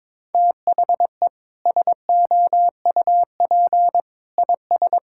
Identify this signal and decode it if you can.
Morse code, heard as a keyed tone, about 21 wpm, 700 Hz. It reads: THE SOUP IS
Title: Sentences from Top 1600 Words 22wpm